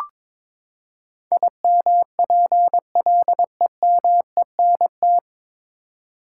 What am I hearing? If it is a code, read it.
Morse code, heard as a keyed tone, about 22 wpm, 700 Hz. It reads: IMPLEMENT
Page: Morse Code Ninja practice 22wpm